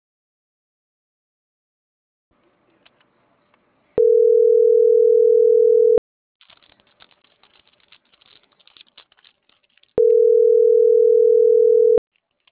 Ringback tone (North America), 440 + 480 Hz.